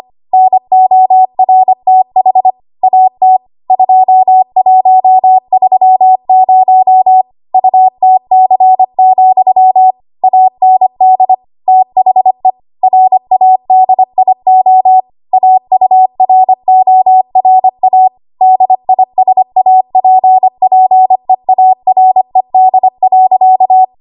Morse code, heard as a keyed tone, about 25 wpm, 750 Hz. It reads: NORTH AT 2130 UTC, AND THE RADIO AURORA DISAPPEARED.
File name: Morse / 25WPM